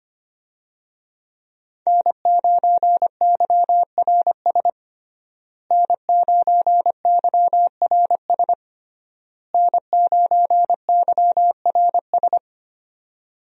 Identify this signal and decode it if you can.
Morse code, heard as a keyed tone, about 25 wpm, 700 Hz. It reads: N9YRS N9YRS N9YRS
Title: USA Call Signs - 5 Characters - Repeated 3x - 25wpm